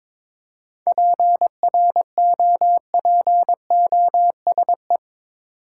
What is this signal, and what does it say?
Morse code, heard as a keyed tone, about 22 wpm, 700 Hz. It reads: PROPOSE
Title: Seven Letter Words - 22wpm